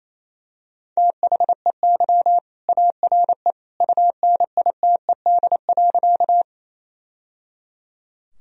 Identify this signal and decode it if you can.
Morse code, heard as a keyed tone, about 28 wpm, 700 Hz. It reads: THEY ARE UNITED.
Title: Sentences from Top 1000 Words - Review Part 4 - 28wpm